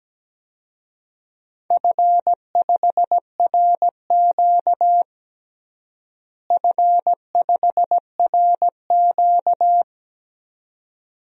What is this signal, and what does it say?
Morse code, heard as a keyed tone, about 17 wpm, 700 Hz. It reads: F5RQ F5RQ
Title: DXCC-100 Call Signs - 4 Characters - Repeated 2x - 17wpm